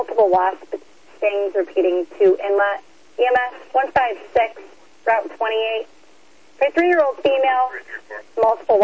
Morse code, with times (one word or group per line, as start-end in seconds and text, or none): none